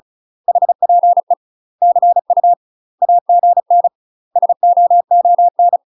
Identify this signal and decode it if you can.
Morse code, heard as a keyed tone, about 36 wpm, 700 Hz. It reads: HPE CU AGN SOON
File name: UK MASTER QSOs TRAINING / 35WPM / UK MASTER QSOs 0042